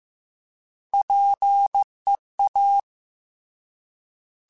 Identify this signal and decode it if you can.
Morse code, heard as a keyed tone, about 15 wpm, 800 Hz. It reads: PEA
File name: Alphabetical / Alphabetical mp3s / Ps @13